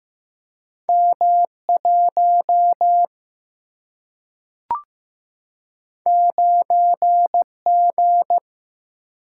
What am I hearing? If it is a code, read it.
Morse code, heard as a keyed tone, about 15 wpm, 700 Hz. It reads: M1 9G
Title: Single-Letter Single-Number 15wpm